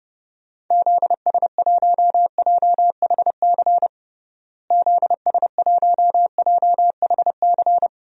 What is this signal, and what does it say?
Morse code, heard as a keyed tone, about 30 wpm, 700 Hz. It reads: ZS1JHC ZS1JHC
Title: DXCC-100 Call Signs - 6 Characters - Repeated 3x - 30wpm